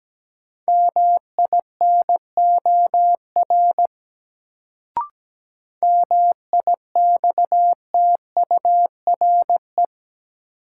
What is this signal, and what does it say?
Morse code, heard as a keyed tone, about 17 wpm, 700 Hz. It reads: MINOR MIXTURE
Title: Sentences from Top 1500 Words - New Word Drill - 17wpm